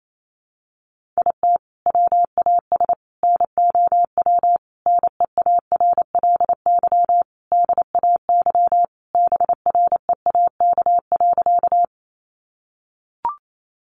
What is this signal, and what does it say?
Morse code, heard as a keyed tone, about 28 wpm, 700 Hz. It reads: IT WAS NOW NEARLY DAYBREAK.
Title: Sentences from Top 1300 Words 28wpm